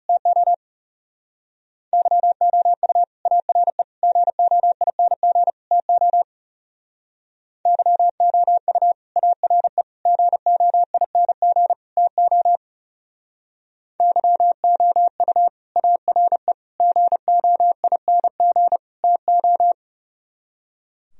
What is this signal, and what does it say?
Morse code, heard as a keyed tone, about 35 wpm, 700 Hz. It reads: TO YOU ARE GOING TO YOU ARE GOING TO YOU ARE GOING TO
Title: Speed-Racing - Sets of 4 Words 30wpm